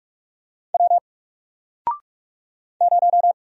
Morse code, heard as a keyed tone, about 44 wpm, 700 Hz. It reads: W 0
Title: Single Letter-Number 45wpm